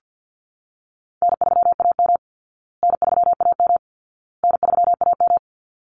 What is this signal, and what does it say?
Morse code, heard as a keyed tone, about 50 wpm, 700 Hz. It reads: N3AM N3AM N3AM
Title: Easy Call Signs Repeated 3x - 50wpm